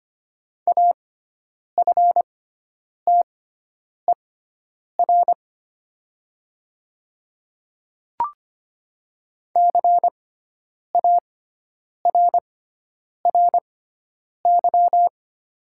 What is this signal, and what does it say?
Morse code, heard as a keyed tone, about 25 wpm, spaced as for 9 wpm, 700 Hz. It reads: AFTER CARRY
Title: 6x Character Spacing - Top 300 Words - 25wpm